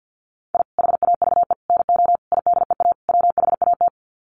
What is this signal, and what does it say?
Morse code, heard as a keyed tone, about 50 wpm, 700 Hz. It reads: I HAVE NO IDEA WHAT
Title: Sets of 5 Words Encore 50wpm